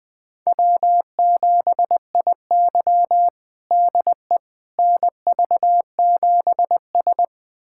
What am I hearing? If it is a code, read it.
Morse code, heard as a keyed tone, about 20 wpm, 700 Hz. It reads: W7IY DE NV7S
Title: USA Callsign DE Callsign - 4 Characters - 20wpm